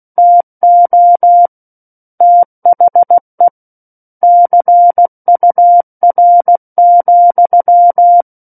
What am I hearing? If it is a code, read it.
Morse code, heard as a keyed tone, about 16 wpm, 700 Hz. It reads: TO THE CUR,
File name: 03-Chapter-03 16 WPM